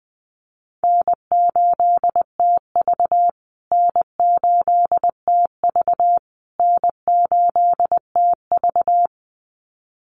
Morse code, heard as a keyed tone, about 20 wpm, 700 Hz. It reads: N8TV N8TV N8TV
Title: USA Call Signs Repeated 3x 20wpm